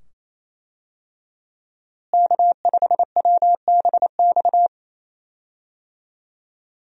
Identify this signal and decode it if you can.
Morse code, heard as a keyed tone, about 28 wpm, 700 Hz. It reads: K5WBX
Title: USA Call Signs - 5 Characters - Repeated 2x - 28wpm